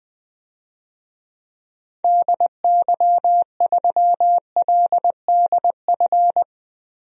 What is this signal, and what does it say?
Morse code, heard as a keyed tone, about 20 wpm, 700 Hz. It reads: DY3LDF
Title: DXCC-100 Call Signs - 6 Characters - 20wpm